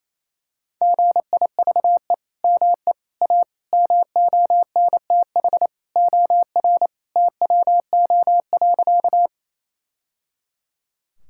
Morse code, heard as a keyed tone, about 28 wpm, 700 Hz. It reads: GIVE ME A MONTH OR TWO.
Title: Sentences from Top 1000 Words - Review Part 5 - 28wpm